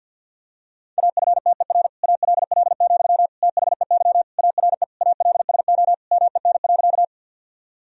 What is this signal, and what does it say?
Morse code, heard as a keyed tone, about 50 wpm, 700 Hz. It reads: AFTER ALL, THEY ARE ALSO MEN.
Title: Sentences from Top 1000 Words - Review Part 1 - 50wpm